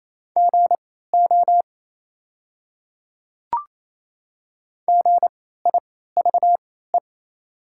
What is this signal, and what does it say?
Morse code, heard as a keyed tone, about 28 wpm, spaced as for 16 wpm, 700 Hz. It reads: GO GIVE
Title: 3x Character Spacing - Top 200 Words - 28wpm